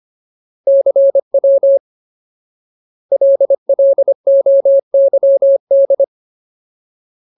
Morse code, heard as a keyed tone, about 25 wpm, 550 Hz. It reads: CW LLOYD
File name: Ultimate Word List / 25WPM Ultimate